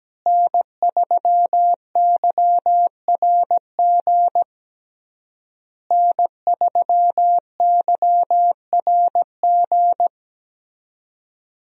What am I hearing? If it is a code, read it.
Morse code, heard as a keyed tone, about 17 wpm, 700 Hz. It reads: N3YRG N3YRG
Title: USA Call Signs - 5 Characters - Repeated 3x - 17wpm